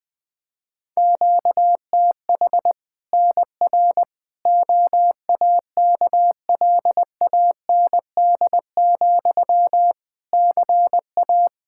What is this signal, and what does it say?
Morse code, heard as a keyed tone, about 20 wpm, 700 Hz. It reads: QTH NR OAKLAND, CA
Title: K1USN SST QSO Practice 20wpm